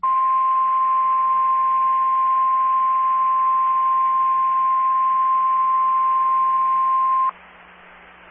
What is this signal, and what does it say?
Emergency Alert System (SAME) tones (the 1050 Hz attention tone). It is heard from 0.8 to 7.1 s.
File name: RR Skywarn / SWIRA Linked Repeaters and SKYWARN Spotters